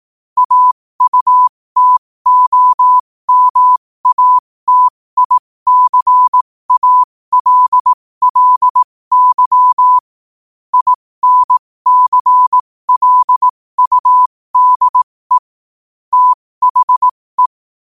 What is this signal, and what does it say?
Morse code, heard as a keyed tone, about 18 wpm, spaced as for 15 wpm, 1000 Hz. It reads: AUTOMATICALLY INCLUDE THE
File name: qotdfiles / 15wpm